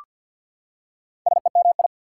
Morse code, heard as a keyed tone, about 48 wpm, 690 Hz. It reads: SEMI
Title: Prefixes and Suffixes 50wpm